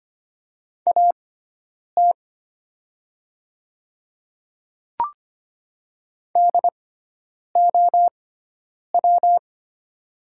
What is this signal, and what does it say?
Morse code, heard as a keyed tone, about 25 wpm, spaced as for 9 wpm, 690 Hz. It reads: AT DOW
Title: 6x Character Spacing - Top 200 Words - 25wpm